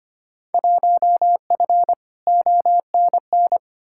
Morse code, heard as a keyed tone, about 25 wpm, 700 Hz. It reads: 1F ONN
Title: Field Day Exchanges Encore - 25wpm